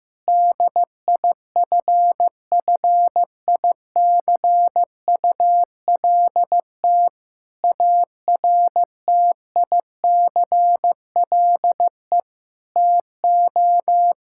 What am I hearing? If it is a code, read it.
Morse code, heard as a keyed tone, about 15 wpm, 700 Hz. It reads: DIFFICULT ARTICLE TO
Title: Sentences from Top 1000 Words - Review Part 5 - 15wpm